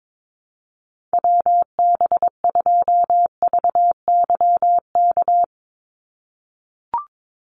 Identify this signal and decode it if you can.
Morse code, heard as a keyed tone, about 22 wpm, 700 Hz. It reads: WB2VYK
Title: Call Signs - 22wpm